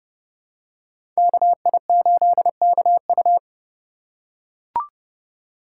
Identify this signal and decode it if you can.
Morse code, heard as a keyed tone, about 32 wpm, 700 Hz. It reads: KI8KU
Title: USA Call Signs - 5 Characters - 6x Character Spacing - 30wpm